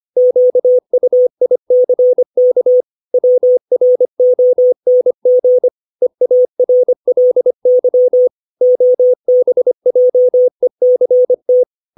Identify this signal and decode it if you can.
Morse code, heard as a keyed tone, about 25 wpm, 500 Hz. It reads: QUICK WRONG EARLY OBJECT